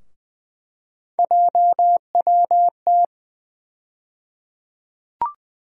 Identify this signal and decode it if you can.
Morse code, heard as a keyed tone, about 20 wpm, 700 Hz. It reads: JWT
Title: USA Call Sign Postfixes - 3 Letters 20wpm